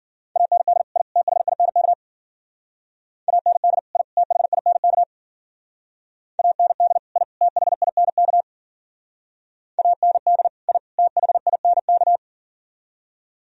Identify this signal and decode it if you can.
Morse code, heard as a keyed tone, about 51 wpm, 700 Hz. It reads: AND I THINK AND I THINK AND I THINK AND I THINK